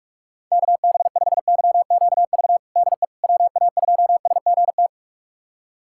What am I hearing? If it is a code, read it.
Morse code, heard as a keyed tone, about 41 wpm, 700 Hz. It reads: KB5YQV DE WA2SGT
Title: USA Callsign DE Callsign 45wpm